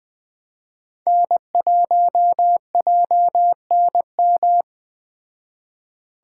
Morse code, heard as a keyed tone, about 20 wpm, 700 Hz. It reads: N1JNM
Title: Speed Racing - USA Call Signs 17wpm